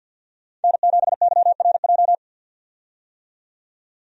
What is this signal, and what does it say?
Morse code, heard as a keyed tone, about 45 wpm, 700 Hz. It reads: N7YRJ
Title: USA Call Signs - 5 Characters - 50wpm